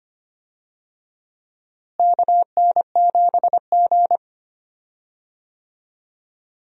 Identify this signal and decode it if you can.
Morse code, heard as a keyed tone, about 25 wpm, 700 Hz. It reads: KN7G
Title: USA Call Signs - 4 Characters - Repeated 2x - 25wpm